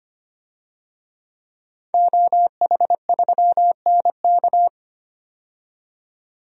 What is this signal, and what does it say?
Morse code, heard as a keyed tone, about 25 wpm, 700 Hz. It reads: OH3NK
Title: DXCC-100 Call Signs - 5 Characters - 25wpm